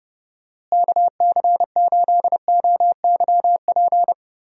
Morse code, heard as a keyed tone, about 30 wpm, 700 Hz. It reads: KC8OYP
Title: USA Call Signs - 6 Characters - Repeated 2x - 30wpm